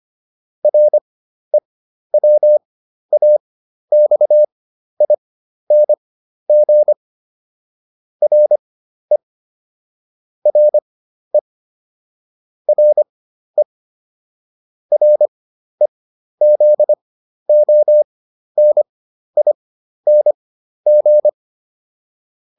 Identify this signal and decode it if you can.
Morse code, heard as a keyed tone, about 25 wpm, spaced as for 12 wpm, 600 Hz. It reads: REWAXING RE RE RE REZONING
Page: fundamental-practice-files